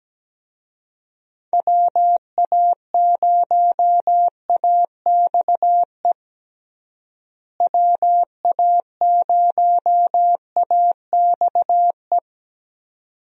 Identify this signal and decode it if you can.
Morse code, heard as a keyed tone, about 17 wpm, 700 Hz. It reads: WA0AXE WA0AXE
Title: USA Call Signs - 6 Characters - Repeated 2x - 17wpm